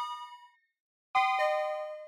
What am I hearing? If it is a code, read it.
There is no code.